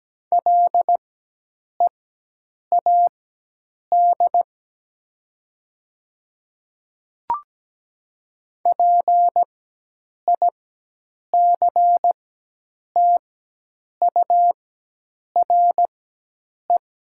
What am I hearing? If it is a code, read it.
Morse code, heard as a keyed tone, about 17 wpm, spaced as for 8 wpm, 700 Hz. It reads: LEAD PICTURE
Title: 4x Character Spacing - Top 500 Words - 17wpm